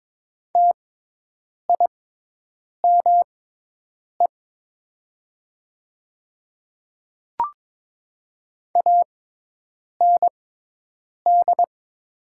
Morse code, heard as a keyed tone, about 22 wpm, spaced as for 8 wpm, 700 Hz. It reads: TIME AND
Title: 6x Character Spacing - Top 100 Words - 22wpm